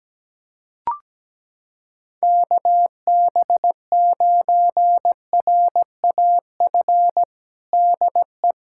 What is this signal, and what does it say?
Morse code, heard as a keyed tone, about 17 wpm, 700 Hz. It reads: KB9RAF DE
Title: USA Callsign DE Callsign 17wpm